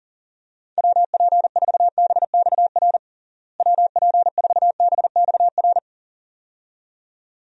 Morse code, heard as a keyed tone, about 37 wpm, 700 Hz. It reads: WP4BXR WP4BXR
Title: USA Call Signs - 6 Characters - Repeated 2x - 40wpm